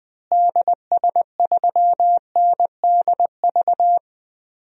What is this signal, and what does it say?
Morse code, heard as a keyed tone, about 20 wpm, 700 Hz. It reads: DS3NDV
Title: DXCC-100 Call Signs - 6 Characters - Repeated 2x - 20wpm